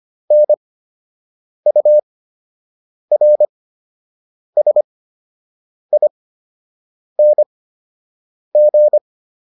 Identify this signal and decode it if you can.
Morse code, heard as a keyed tone, about 25 wpm, spaced as for 7 wpm, 600 Hz. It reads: NURSING